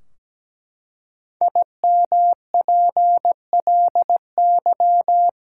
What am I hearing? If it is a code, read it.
Morse code, heard as a keyed tone, about 17 wpm, 700 Hz. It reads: IMPLY
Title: Five Letter Words - 17wpm